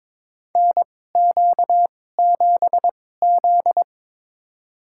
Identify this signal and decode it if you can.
Morse code, heard as a keyed tone, about 22 wpm, spaced as for 16 wpm, 700 Hz. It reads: NQ7Z
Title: USA Call Signs - 2x Character Spacing - 22wpm